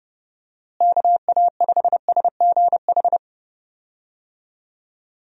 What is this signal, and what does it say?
Morse code, heard as a keyed tone, about 31 wpm, 700 Hz. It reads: KA5SGH
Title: USA Call Signs - 6 Characters - Repeated 2x - 30wpm